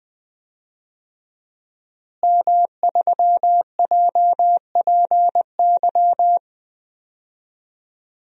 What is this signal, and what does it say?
Morse code, heard as a keyed tone, about 20 wpm, 700 Hz. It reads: M3JPY
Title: DXCC-100 Call Signs - 5 Characters - Repeated 3x - 20wpm